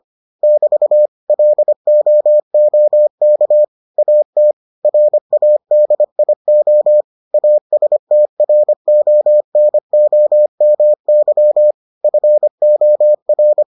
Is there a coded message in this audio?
Yes: Morse code from 0.4 to 13.6 s, heard as a keyed tone, about 25 wpm, 600 Hz. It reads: <BT> LOOK AT RADIO ASTRONOMY FOR